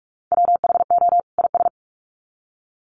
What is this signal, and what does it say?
Morse code, heard as a keyed tone, about 44 wpm, 700 Hz. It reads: WHO IS
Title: Sets of 2 Words 45wpm